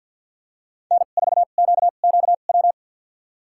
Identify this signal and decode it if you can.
Morse code, heard as a keyed tone, about 51 wpm, spaced as for 36 wpm, 700 Hz. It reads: N4QQW
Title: USA Call Signs - 5 Characters - 2x Character Spacing - 50wpm